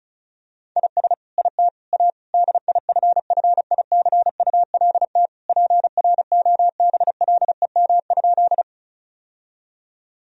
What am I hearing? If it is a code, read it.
Morse code, heard as a keyed tone, about 34 wpm, 700 Hz. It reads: IS IT A DIFFICULT PROBLEM?
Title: Sentences from Top 1000 Words - Review Part 2 - 35wpm